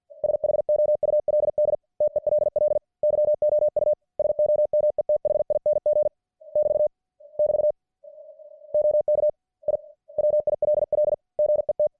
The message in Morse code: SHOULD TELL YOU SOMETHING <BT> <BT> OK I WILL GET